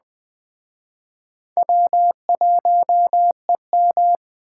Morse code, heard as a keyed tone, about 21 wpm, 700 Hz. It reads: W1EM
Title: Speed Racing - USA 1x2 Call Signs 17wpm